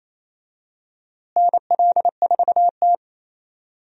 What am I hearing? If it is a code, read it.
Morse code, heard as a keyed tone, about 28 wpm, 700 Hz. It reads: NL4T
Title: USA Call Signs - 4 Characters - Repeated 3x - 28wpm